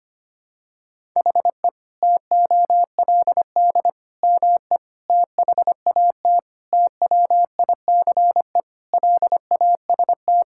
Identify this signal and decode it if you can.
Morse code, heard as a keyed tone, about 25 wpm, 700 Hz. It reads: HE TOLD ME THAT TWICE LAST
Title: Sentences from Top 1600 Words 25wpm